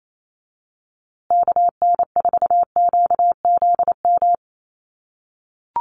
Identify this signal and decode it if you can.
Morse code, heard as a keyed tone, about 28 wpm, 700 Hz. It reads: KN4QZM
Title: USA Call Signs Repeated 3x 28wpm